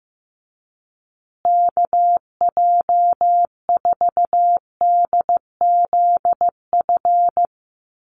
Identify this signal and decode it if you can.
Morse code, heard as a keyed tone, about 15 wpm, 700 Hz. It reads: KJ4DZF
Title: USA Call Signs Repeated 2x 15wpm